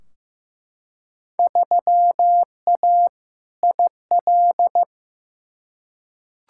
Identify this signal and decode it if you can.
Morse code, heard as a keyed tone, about 15 wpm, 700 Hz. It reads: 3A IL
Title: Field Day Exchanges - 15wpm